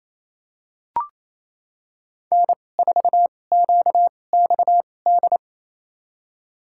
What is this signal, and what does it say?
Morse code, heard as a keyed tone, about 28 wpm, spaced as for 20 wpm, 700 Hz. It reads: N4QXD
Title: USA Call Signs - 5 Characters - 2x Character Spacing - 28wpm